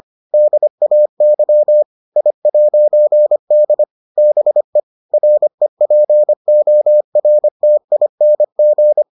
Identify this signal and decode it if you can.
Morse code, heard as a keyed tone, about 25 wpm, 600 Hz. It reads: DAY I'D BE REPORTING